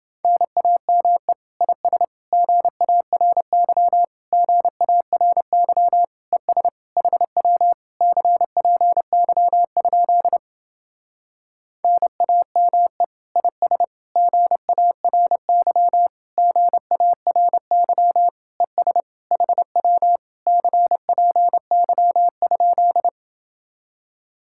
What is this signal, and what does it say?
Morse code, heard as a keyed tone, about 28 wpm, 700 Hz. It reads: NAME IS GARY GARY ES HW CPY? NAME IS GARY GARY ES HW CPY?